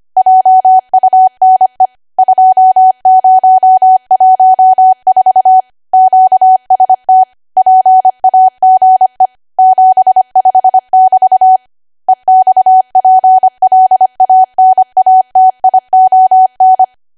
Morse code, heard as a keyed tone, about 25 wpm, 750 Hz. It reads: JUNE 2014 QST PAGE 75 <BT> EXPLANATION
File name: Archive / 25 WPM / 25WPM